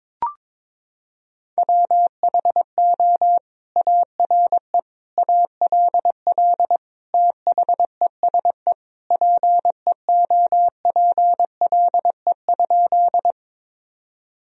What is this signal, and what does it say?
Morse code, heard as a keyed tone, about 22 wpm, 700 Hz. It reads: WHO ARE ALL THESE PEOPLE?
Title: Sentences from Top 1000 Words - Review Part 1 - 22wpm